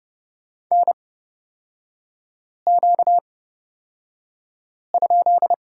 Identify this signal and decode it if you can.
Morse code, heard as a keyed tone, about 30 wpm, spaced as for 5 wpm, 700 Hz. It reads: NQ?
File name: Lesson 38e - Letter q - Three Characters 30wpm